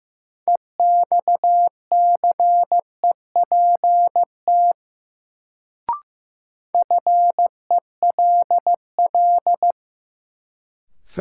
Morse code, heard as a keyed tone, about 15 wpm, 700 Hz. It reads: EXCEPT FELL
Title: Sentences from Top 1600 Words - New Word Drill - 15wpm